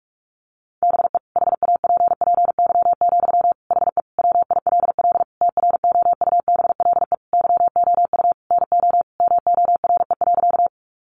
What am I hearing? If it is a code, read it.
Morse code, heard as a keyed tone, about 44 wpm, 700 Hz. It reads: BE HAPPY, HE WILL TROUBLE YOU NO MORE.
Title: Sentences from Top 1200 Words 45wpm